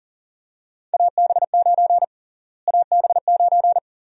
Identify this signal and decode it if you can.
Morse code, heard as a keyed tone, about 40 wpm, 700 Hz. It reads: AB9 AB9
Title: USA Call Sign Prefixes - Repeated 2x - 40wpm